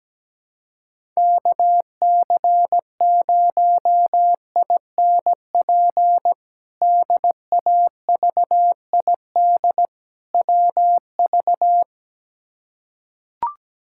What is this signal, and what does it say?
Morse code, heard as a keyed tone, about 17 wpm, 700 Hz. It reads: KC0INP DAVID WV
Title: K1USN SST QSO Practice - 17wpm